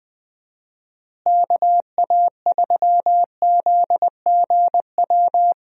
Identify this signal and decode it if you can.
Morse code, heard as a keyed tone, about 21 wpm, 700 Hz. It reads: KA3ZGW
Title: USA Call Signs - 6 Characters - Repeated 3x - 20wpm